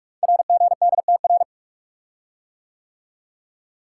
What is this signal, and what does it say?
Morse code, heard as a keyed tone, about 44 wpm, 690 Hz. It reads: RGDTR
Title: Code Groups - 5 Letters - 45wpm